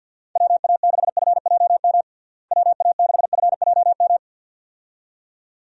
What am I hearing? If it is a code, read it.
Morse code, heard as a keyed tone, about 53 wpm, 700 Hz. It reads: WA6FJM WA6FJM
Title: USA Call Signs - 6 Characters - Repeated 2x - 50wpm